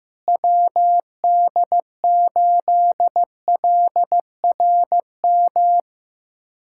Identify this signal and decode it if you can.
Morse code, heard as a keyed tone, about 15 wpm, 700 Hz. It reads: WD8LRM
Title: USA Callsign DE Callsign - 6 Characters - 15wpm